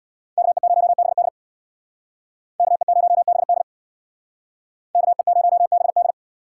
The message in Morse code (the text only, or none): KE0BD KE0BD KE0BD